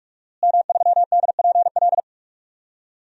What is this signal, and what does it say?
Morse code, heard as a keyed tone, about 46 wpm, 700 Hz. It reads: M3DPL
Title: DXCC-100 Call Signs - 5 Characters - Repeated 3x - 45wpm